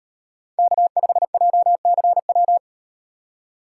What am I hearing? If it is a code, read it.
Morse code, heard as a keyed tone, about 38 wpm, 700 Hz. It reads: K5JCW